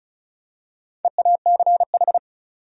Morse code, heard as a keyed tone, about 34 wpm, 700 Hz. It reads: EACH
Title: Top 100 Words Repeated 35wpm